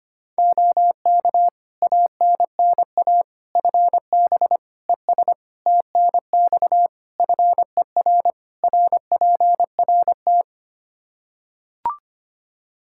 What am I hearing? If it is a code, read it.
Morse code, heard as a keyed tone, about 25 wpm, 700 Hz. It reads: OK ANNA FB ES TNX FER RPRT
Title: Practice QSO - Exchange 2 - Info - 25wpm